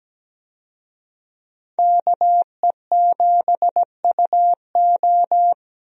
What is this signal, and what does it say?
Morse code, heard as a keyed tone, about 17 wpm, 700 Hz. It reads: KE7UO
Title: USA Call Signs - 5 Characters - Repeated 3x - 17wpm